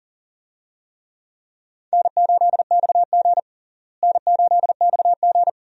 Morse code, heard as a keyed tone, about 40 wpm, 700 Hz. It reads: N8XG N8XG
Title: USA Call Signs - 4 Characters - Repeated 3x - 40wpm